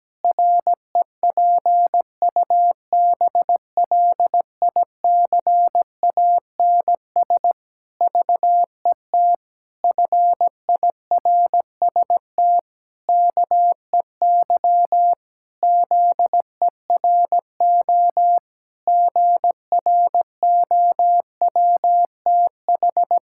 Morse code, heard as a keyed tone, about 17 wpm, 700 Hz. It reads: REPUBLICANS VET FIRST KEY ZERO GROWTH
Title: Pangram Challenge v2 - 17wpm